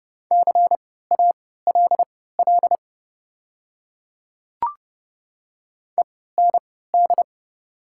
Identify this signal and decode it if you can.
Morse code, heard as a keyed tone, about 30 wpm, spaced as for 17 wpm, 690 Hz. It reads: CALL END
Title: 3x Character Spacing - Top 200 Words - 30wpm